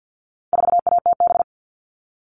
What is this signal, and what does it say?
Morse code, heard as a keyed tone, about 50 wpm, 700 Hz. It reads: 4ATB
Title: All Cut Numbers - 4-Digit 50wpm